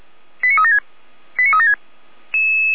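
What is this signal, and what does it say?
Five-tone selective call (ZVEI), heard as quick runs of five tones, tones of about 80 ms. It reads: 89376 89376
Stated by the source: ZVEI five-tone selective call 89376 received on an SDR.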